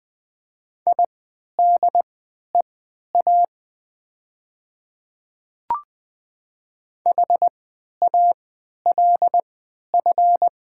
Morse code, heard as a keyed tone, about 20 wpm, spaced as for 11 wpm, 700 Hz. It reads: IDEA HALF
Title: 3x Character Spacing - Top 500 Words - 20wpm